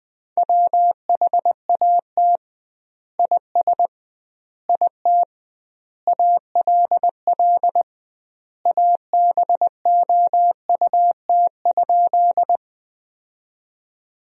Morse code, heard as a keyed tone, about 20 wpm, 700 Hz. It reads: WHAT IS IT ALL ABOUT?
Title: 2x Word-Spacing - Sentences from Top 100 Words - 20wpm